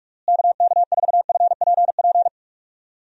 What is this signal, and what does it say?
Morse code, heard as a keyed tone, about 46 wpm, 700 Hz. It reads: KK4FPP
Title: USA Call Signs - 6 Characters - Repeated 3x - 45wpm